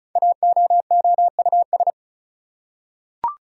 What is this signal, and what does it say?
Morse code, heard as a keyed tone, about 36 wpm, 700 Hz. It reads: AOOUS
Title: Code Groups - 5 Letters - 35wpm